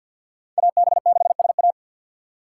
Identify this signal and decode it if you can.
Morse code, heard as a keyed tone, about 53 wpm, 700 Hz. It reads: AB6SA
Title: USA Call Signs - 6x Character Spacing - 50wpm